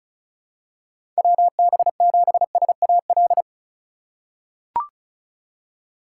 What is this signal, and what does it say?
Morse code, heard as a keyed tone, about 34 wpm, 700 Hz. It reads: WB7SAL
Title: USA Call Signs - 6 Characters - 2x Character Spacing - 35wpm